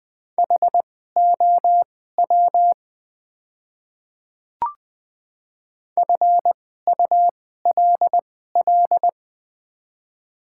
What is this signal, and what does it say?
Morse code, heard as a keyed tone, about 20 wpm, spaced as for 15 wpm, 700 Hz. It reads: HOW FULL
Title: 2x Character Spacing - Top 500 Words - 20wpm